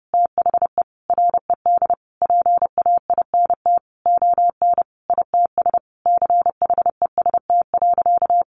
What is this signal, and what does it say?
Morse code, heard as a keyed tone, about 28 wpm, 700 Hz. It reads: THE RED PAINT ON ITS CHEST.
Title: A Sound Of Thunder 30wpm